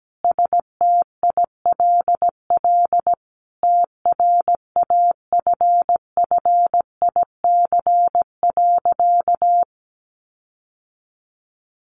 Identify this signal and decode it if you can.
Morse code, heard as a keyed tone, about 17 wpm, 700 Hz. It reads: STILL TRAFFIC.
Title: Sentences from Top 1300 Words 17wpm